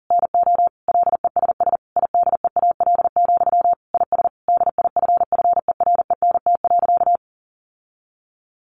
Morse code, heard as a keyed tone, about 40 wpm, 700 Hz. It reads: NO LESS IDEAL, IS DIFFERENT.
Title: Sentences from Top 1400 Words 40wpm